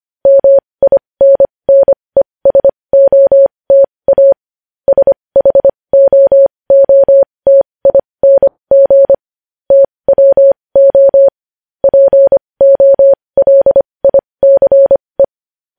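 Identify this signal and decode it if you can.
Morse code, heard as a keyed tone, about 25 wpm, spaced as for 20 wpm, 550 Hz. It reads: MINNESOTA SHOOTING TWO POLICE
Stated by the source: Morse code news headlines at 20 wpm.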